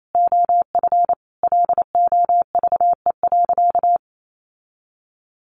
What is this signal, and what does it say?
Morse code, heard as a keyed tone, about 28 wpm, 700 Hz. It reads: OF LOVE.
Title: Sentences from Top 1400 Words 28wpm